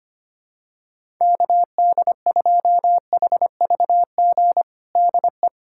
Morse code, heard as a keyed tone, about 25 wpm, 700 Hz. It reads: KD2HVG DE
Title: USA Callsign DE Callsign - 6 Characters - 25wpm